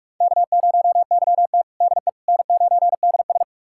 Morse code, heard as a keyed tone, about 46 wpm, 700 Hz. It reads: K0YT DE N9DS
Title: USA Callsign DE Callsign - 4 Characters - 45wpm